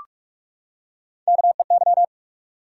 Morse code, heard as a keyed tone, about 44 wpm, 700 Hz. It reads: KEY
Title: Three Letter Words - 45wpm